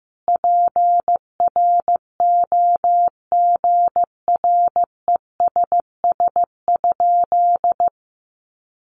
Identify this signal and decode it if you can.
Morse code, heard as a keyed tone, about 15 wpm, 700 Hz. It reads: PROGRESS?
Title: Sentences from Top 1400 Words 15wpm